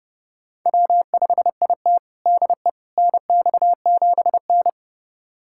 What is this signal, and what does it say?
Morse code, heard as a keyed tone, about 30 wpm, 700 Hz. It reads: W5IT DE NX7N
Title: USA Callsign DE Callsign - 4 Characters - 30wpm